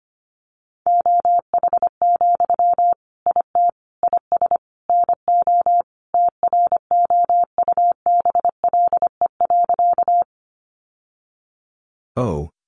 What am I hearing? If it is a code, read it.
Morse code, heard as a keyed tone, about 25 wpm, 700 Hz. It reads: OH, IT IS NO TROUBLE.
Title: Sentences from Top 1200 Words 25wpm